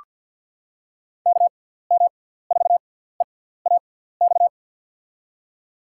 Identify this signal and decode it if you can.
Morse code, heard as a keyed tone, about 54 wpm, spaced as for 17 wpm, 700 Hz. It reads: KM4EAX
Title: USA Call Signs - 6 Characters - 6x Character Spacing - 50wpm